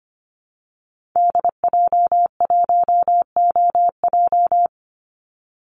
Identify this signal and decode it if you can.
Morse code, heard as a keyed tone, about 25 wpm, 700 Hz. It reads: DJ1OJ
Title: Call Signs Repeated 2x - 25wpm